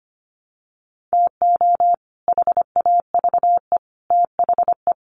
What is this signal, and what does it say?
Morse code, heard as a keyed tone, about 25 wpm, 700 Hz. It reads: TO HAVE THE
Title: Sets of 3 Words 25wpm